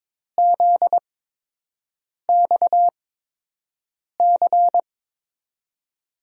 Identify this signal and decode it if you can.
Morse code, heard as a keyed tone, about 22 wpm, spaced as for 6 wpm, 700 Hz. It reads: ZXC
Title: USA Call Signs - 6 Characters - 8x Character Spacing - 22wpm